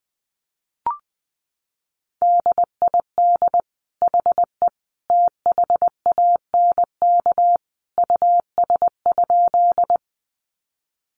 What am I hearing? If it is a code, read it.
Morse code, heard as a keyed tone, about 20 wpm, 700 Hz. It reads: DID HE THANK US?
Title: Sentences from Top 600 Words 20wpm